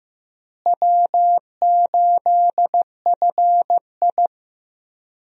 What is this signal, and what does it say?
Morse code, heard as a keyed tone, about 15 wpm, 700 Hz. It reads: W8FI
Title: USA Call Signs - 4 Characters - Repeated 3x - 15wpm